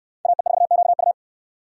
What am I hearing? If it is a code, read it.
Morse code, heard as a keyed tone, about 69 wpm, 700 Hz. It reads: AE4QU